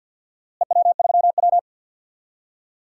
Morse code, heard as a keyed tone, about 52 wpm, 700 Hz. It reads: EW3W
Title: DXCC-100 Call Signs - 4 Characters - Repeated 2x - 50wpm